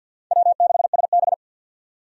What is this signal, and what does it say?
Morse code, heard as a keyed tone, about 50 wpm, 700 Hz. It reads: W6SB
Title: USA Call Signs - 4 Characters - 50wpm